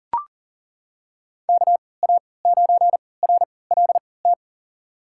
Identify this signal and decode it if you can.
Morse code, heard as a keyed tone, about 40 wpm, spaced as for 23 wpm, 700 Hz. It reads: KA9RLT
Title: USA Call Signs - 3x Character Spacing - 40wpm